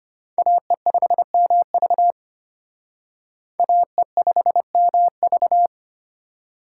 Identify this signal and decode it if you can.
Morse code, heard as a keyed tone, about 32 wpm, 700 Hz. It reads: AE5MV AE5MV